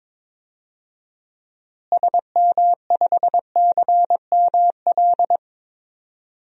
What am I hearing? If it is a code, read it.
Morse code, heard as a keyed tone, about 22 wpm, 700 Hz. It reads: SM5CML
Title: DXCC-100 Call Signs - 6 Characters - Repeated 3x - 22wpm